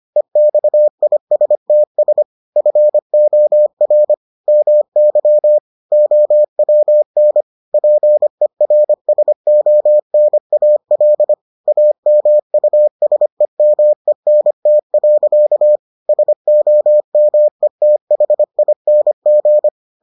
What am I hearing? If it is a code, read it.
Morse code, heard as a keyed tone, about 25 wpm, 600 Hz. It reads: EXISTS FOR MY OWN PERSONAL AMUSEMENT. SOMETHING